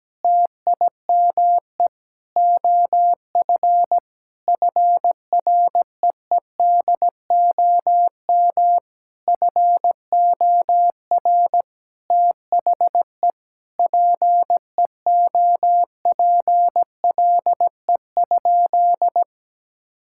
Morse code, heard as a keyed tone, about 17 wpm, 700 Hz. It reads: TIME OF FREEDOM FOR THE PEOPLE?
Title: Sentences from Top 1000 Words - Review Part 5 - 17wpm